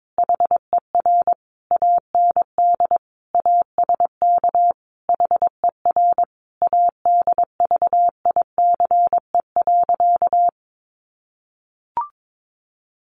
Morse code, heard as a keyed tone, about 22 wpm, 700 Hz. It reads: HER AND ASK HER ADVICE.
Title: Sentences from Top 1100 Words 22wpm